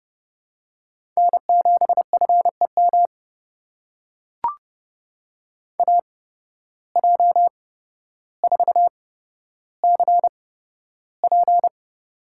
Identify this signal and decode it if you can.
Morse code, heard as a keyed tone, about 30 wpm, spaced as for 13 wpm, 700 Hz. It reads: N7FEM AJ4CP
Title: USA Call Signs - 5 Characters - 8x Character Spacing - 30wpm